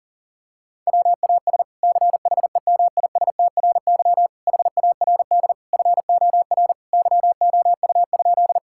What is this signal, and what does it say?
Morse code, heard as a keyed tone, about 40 wpm, 700 Hz. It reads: WAS CHEMISTRY HARD FOR YOU?
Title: Sentences from Top 1000 Words - Review Part 6 - 40wpm